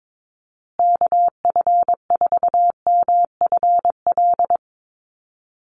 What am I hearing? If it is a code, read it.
Morse code, heard as a keyed tone, about 22 wpm, 700 Hz. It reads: KF4MFL
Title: USA Call Signs Repeated 2x 22wpm